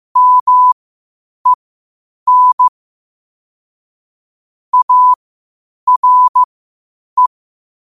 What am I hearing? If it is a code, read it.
Morse code, heard as a keyed tone, about 15 wpm, spaced as for 8 wpm, 1000 Hz. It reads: MEN ARE
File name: qotdfiles / 8wpm